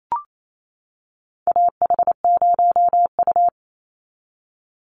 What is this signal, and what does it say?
Morse code, heard as a keyed tone, about 28 wpm, 700 Hz. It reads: AH0U
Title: Call Signs - 28wpm